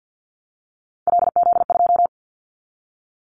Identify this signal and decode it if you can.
Morse code, heard as a keyed tone, about 50 wpm, 700 Hz. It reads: LZ2
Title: Callsign Prefixes for Top 50 DXCC - 50wpm